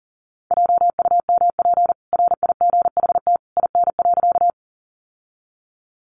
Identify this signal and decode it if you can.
Morse code, heard as a keyed tone, about 40 wpm, 700 Hz. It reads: JUMP RIGHT IN.
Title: Sentences from Top 1300 Words 40wpm